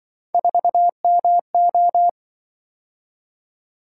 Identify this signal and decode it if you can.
Morse code, heard as a keyed tone, about 24 wpm, 700 Hz. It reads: 4MO
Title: Speed-Racing - Three Letter-Number Combinations - 20wpm